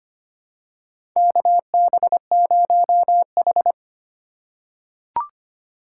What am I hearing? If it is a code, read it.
Morse code, heard as a keyed tone, about 25 wpm, 700 Hz. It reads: KB0H
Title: USA Call Signs - 4 Characters - 4x Character Spacing - 25wpm